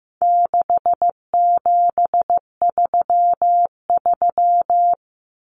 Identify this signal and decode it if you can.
Morse code, heard as a keyed tone, about 15 wpm, 700 Hz. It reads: 6733
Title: Numbers - 4-Digit 15wpm